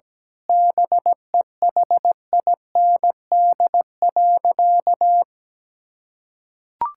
Morse code, heard as a keyed tone, about 17 wpm, 700 Hz. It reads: BEHIND.
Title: Sentences from Top 1600 Words 17wpm